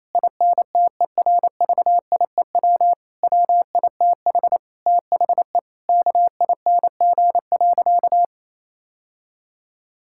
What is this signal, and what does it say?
Morse code, heard as a keyed tone, about 28 wpm, 700 Hz. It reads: INTERVIEW WITH THE KING.
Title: Sentences from Top 1600 Words 28wpm